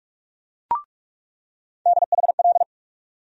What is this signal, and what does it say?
Morse code, heard as a keyed tone, about 47 wpm, 700 Hz. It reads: DHL
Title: USA Call Sign Postfixes - 3 Letters 45wpm